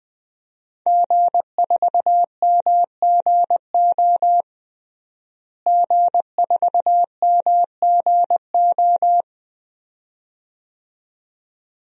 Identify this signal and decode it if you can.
Morse code, heard as a keyed tone, about 20 wpm, 700 Hz. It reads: G4MGO G4MGO
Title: DXCC-100 Call Signs - 5 Characters - Repeated 2x - 20wpm